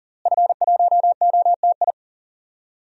Morse code, heard as a keyed tone, about 40 wpm, 700 Hz. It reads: F1OTI
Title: DXCC-100 Call Signs - 5 Characters - Repeated 2x - 40wpm